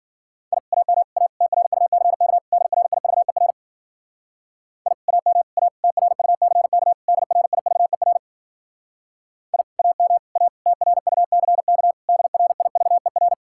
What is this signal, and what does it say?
Morse code, heard as a keyed tone, about 50 wpm, 700 Hz. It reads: I AM A TRUCK DRIVER I AM A TRUCK DRIVER I AM A TRUCK DRIVER